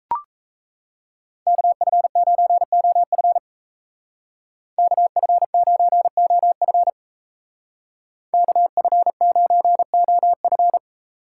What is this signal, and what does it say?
Morse code, heard as a keyed tone, about 33 wpm, 700 Hz. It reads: KF9OF KF9OF KF9OF
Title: Speed Racing - USA 2x2 Call Signs 28wpm